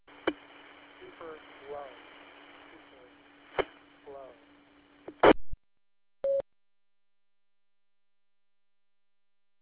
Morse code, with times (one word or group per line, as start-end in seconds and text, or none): none